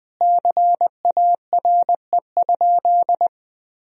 Morse code, heard as a keyed tone, about 20 wpm, 700 Hz. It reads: CARE?
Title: Sentences from Top 1000 Words - Review Part 1 - 20wpm